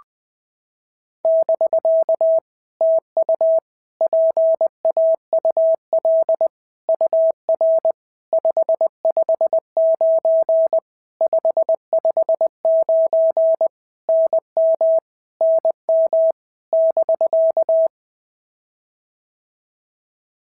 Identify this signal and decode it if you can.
Morse code, heard as a keyed tone, about 20 wpm, 650 Hz. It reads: <BK> TU PAUL UR 559 559 NM NM <BK>
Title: POTA QSO Practice - 20wpm